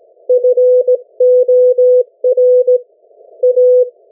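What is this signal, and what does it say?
Morse code, heard as a keyed tone, about 17 wpm, 500 Hz. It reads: FOR A
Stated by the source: Morse code from the national amateur radio CW net of New Zealand, which meets on 3535.0 kHz or 7030 kHz.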